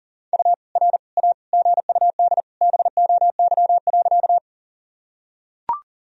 Morse code, heard as a keyed tone, about 40 wpm, 700 Hz. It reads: U R A GUD BOY.
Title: Sentences from Top 200 Words and Abbreviations 40wpm